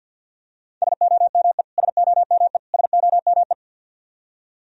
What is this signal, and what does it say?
Morse code, heard as a keyed tone, about 50 wpm, 700 Hz. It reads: SOME SOME SOME
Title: Top 100 Words Repeated 50wpm